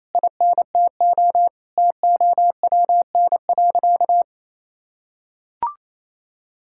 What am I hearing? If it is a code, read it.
Morse code, heard as a keyed tone, about 28 wpm, 700 Hz. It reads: INTO TOWN.
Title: Sentences from Top 1000 Words - Review Part 6 - 28wpm